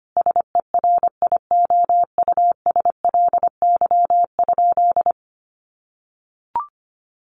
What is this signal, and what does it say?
Morse code, heard as a keyed tone, about 25 wpm, 700 Hz. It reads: SERIOUSLY?
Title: Sentences from Top 1200 Words 25wpm